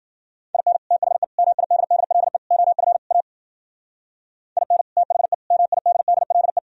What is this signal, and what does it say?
Morse code, heard as a keyed tone, about 48 wpm, 700 Hz. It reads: IN THE MIDDLE OF A IN THE MIDDLE
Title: Speed-Racing - Sets of 5 Words 40wpm